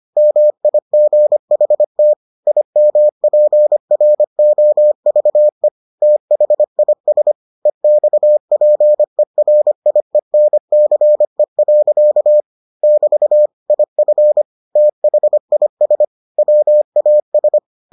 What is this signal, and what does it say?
Morse code, heard as a keyed tone, about 25 wpm, 600 Hz. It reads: MIGHT IMPROVE THIS EXPERIENCE. <BT> IF THIS WAS